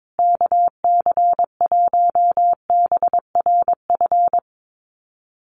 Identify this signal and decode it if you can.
Morse code, heard as a keyed tone, about 22 wpm, 700 Hz. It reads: KC1BRF
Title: USA Call Signs Repeated 2x 22wpm